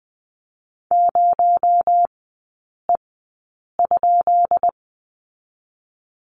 Morse code, heard as a keyed tone, about 20 wpm, spaced as for 8 wpm, 700 Hz. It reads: 0E?
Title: 218 Lesson 35e - Number 0 - Three Characters 20wpm